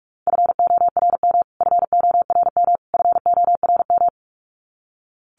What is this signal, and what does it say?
Morse code, heard as a keyed tone, about 44 wpm, 700 Hz. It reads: FORM FORM FORM
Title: Top 500 Words Repeated 45wpm